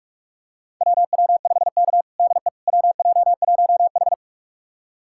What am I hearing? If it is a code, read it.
Morse code, heard as a keyed tone, about 46 wpm, 700 Hz. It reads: WW5K DE WJ1H
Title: USA Callsign DE Callsign - 4 Characters - 45wpm